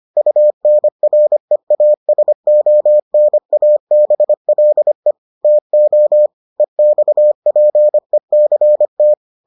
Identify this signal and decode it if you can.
Morse code, heard as a keyed tone, about 25 wpm, 600 Hz. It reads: UNREASONABLE TO EXPECT